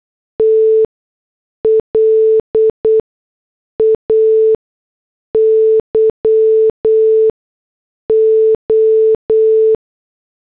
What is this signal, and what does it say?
Morse code, heard as a keyed tone, about 8 wpm, spaced as for 6 wpm, 440 Hz. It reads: T LAYO
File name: morse / headlines